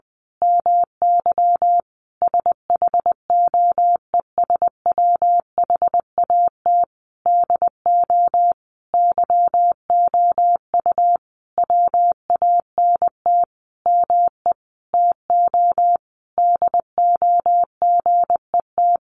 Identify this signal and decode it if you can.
Morse code, heard as a keyed tone, about 20 wpm, 700 Hz. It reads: MY SHOES WHAT DO YOU WANT ME TO DO GET
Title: A Sound Of Thunder 20wpm